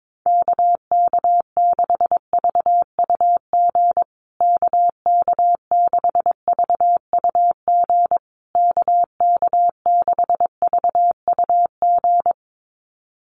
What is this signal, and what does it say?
Morse code, heard as a keyed tone, about 22 wpm, 700 Hz. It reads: KK6VUG KK6VUG KK6VUG
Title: USA Call Signs Repeated 3x 22wpm